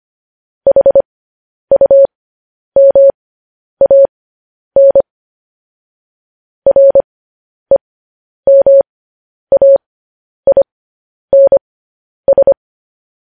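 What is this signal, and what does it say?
Morse code, heard as a keyed tone, about 25 wpm, spaced as for 10 wpm, 550 Hz. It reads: HUMAN REMAINS
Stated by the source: Morse code news headlines broadcast, opening with vvv.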